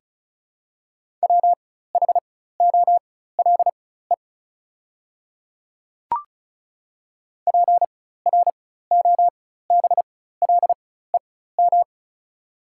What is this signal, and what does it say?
Morse code, heard as a keyed tone, about 36 wpm, spaced as for 16 wpm, 700 Hz. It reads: WHOLE PROBLEM
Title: 4x Character Spacing - Top 500 Words - 35wpm